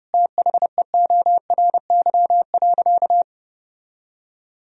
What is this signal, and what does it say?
Morse code, heard as a keyed tone, about 30 wpm, 700 Hz. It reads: THEORY.
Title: Sentences from Top 1000 Words - Review Part 3 - 30wpm